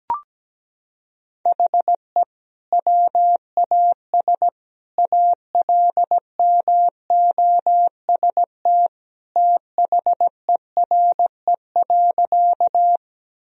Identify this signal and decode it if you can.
Morse code, heard as a keyed tone, about 17 wpm, 700 Hz. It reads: HE WAS ALMOST THERE.
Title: Sentences from Top 1000 Words - Review Part 4 - 17wpm